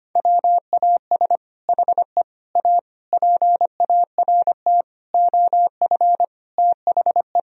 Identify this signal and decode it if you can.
Morse code, heard as a keyed tone, about 25 wpm, 700 Hz. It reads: WAS HE A PART OF THE
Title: Sentences from Top 1000 Words - Review Part 4 - 25wpm